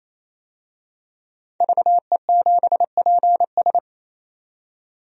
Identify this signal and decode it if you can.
Morse code, heard as a keyed tone, about 28 wpm, 700 Hz. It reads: VE7PS
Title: DXCC-100 Call Signs - 5 Characters - 28wpm